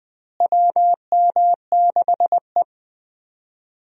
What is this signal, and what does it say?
Morse code, heard as a keyed tone, about 20 wpm, 700 Hz. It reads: WM6E
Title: USA Call Signs - 4 Characters - Repeated 2x - 20wpm